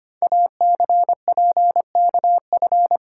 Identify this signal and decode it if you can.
Morse code, heard as a keyed tone, about 25 wpm, 690 Hz. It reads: ACPKF
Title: Code Groups - 5 Letter:Numbers - 25wpm